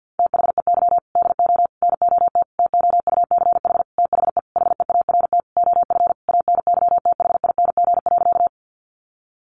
Morse code, heard as a keyed tone, about 50 wpm, 700 Hz. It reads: THEY DO NOT TOUCH THE HEART OF ANYTHING.
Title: Sentences from Top 1200 Words 50wpm